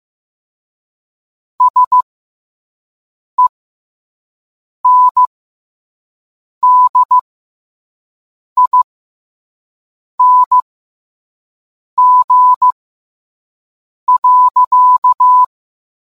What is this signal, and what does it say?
Morse code, heard as a keyed tone, about 15 wpm, spaced as for 5 wpm, 1000 Hz. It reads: SENDING.